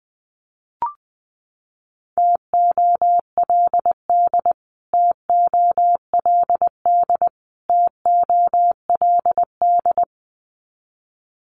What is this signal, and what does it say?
Morse code, heard as a keyed tone, about 20 wpm, 700 Hz. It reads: TOLD TOLD TOLD
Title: Top 500 Words Repeated 20wpm